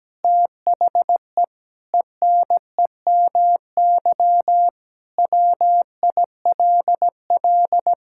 Morse code, heard as a keyed tone, about 17 wpm, 700 Hz. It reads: THE ENEMY WILL
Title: Pangram Challenge v2 - 17wpm